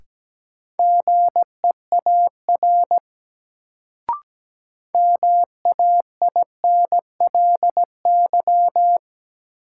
Morse code, heard as a keyed tone, about 17 wpm, 700 Hz. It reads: GEAR MAINLY
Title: Sentences from Top 1400 Words - New Word Drill - 17wpm